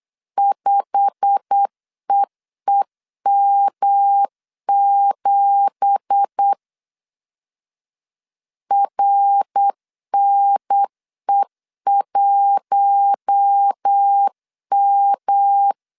Morse code, heard as a keyed tone, about 8 wpm, 800 Hz. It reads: 5EEM7 RNE1M
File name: LEZIONI / 10 SIGNS / 10C DOT 8wpm 3649